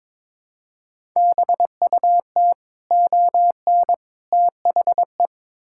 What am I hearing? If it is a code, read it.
Morse code, heard as a keyed tone, about 21 wpm, 700 Hz. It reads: BUT ON THE
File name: practice / 22wpm / Build A Phrase - 5-Word Phrases - 22wpm